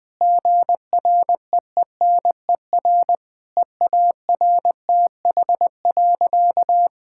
Morse code, heard as a keyed tone, about 20 wpm, 700 Hz. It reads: GREENER EARTH.
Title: Difficult Sentences with Hints - 20wpm